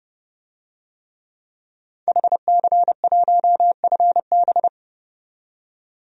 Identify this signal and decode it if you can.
Morse code, heard as a keyed tone, about 30 wpm, 700 Hz. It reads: HC1FB
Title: DXCC-100 Call Signs - 5 Characters - 30wpm